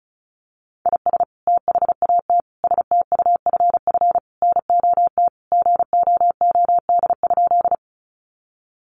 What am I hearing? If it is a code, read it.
Morse code, heard as a keyed tone, about 34 wpm, 700 Hz. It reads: IS THAT STUFF NOT GOOD?
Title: Sentences from Top 1400 Words 35wpm